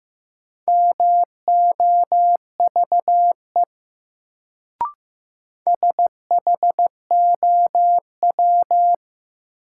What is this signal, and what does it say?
Morse code, heard as a keyed tone, about 15 wpm, 700 Hz. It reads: MOVE SHOW
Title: Sentences from Top 200 Words - New Word Drill - 15wpm